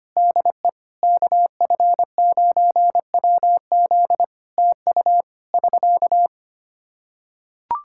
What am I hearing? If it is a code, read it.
Morse code, heard as a keyed tone, about 25 wpm, 700 Hz. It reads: DE KF9WZ TU <SK>
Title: K1USN SST QSO Practice 25wpm